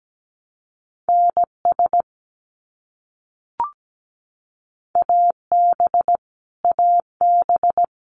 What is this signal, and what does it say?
Morse code, heard as a keyed tone, about 17 wpm, 700 Hz. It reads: NS AB AB
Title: Canadian Provinces and Territories - Repeated 3x - 17wpm